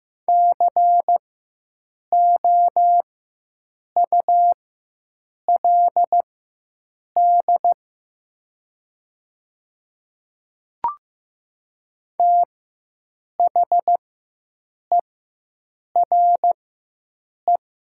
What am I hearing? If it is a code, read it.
Morse code, heard as a keyed tone, about 15 wpm, spaced as for 7 wpm, 700 Hz. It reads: COULD THERE
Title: 4x Character Spacing - Top 100 Words - 15wpm